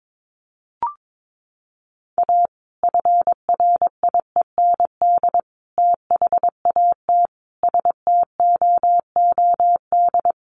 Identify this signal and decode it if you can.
Morse code, heard as a keyed tone, about 22 wpm, 700 Hz. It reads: A FRIEND THAT STOOD
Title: Sentences from Top 1500 Words 22wpm